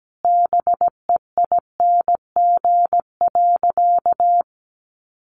Morse code, heard as a keyed tone, about 17 wpm, 700 Hz. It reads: BEING.
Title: Sentences from Top 1500 Words 17wpm